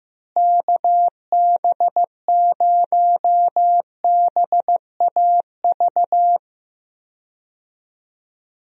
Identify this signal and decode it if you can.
Morse code, heard as a keyed tone, about 15 wpm, 700 Hz. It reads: KB0BAV
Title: USA Call Signs - 6 Characters - Repeated 2x - 15wpm